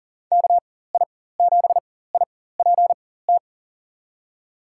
Morse code, heard as a keyed tone, about 40 wpm, spaced as for 19 wpm, 700 Hz. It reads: KI7IPT
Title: USA Call Signs - 6 Characters - 4x Character Spacing - 40wpm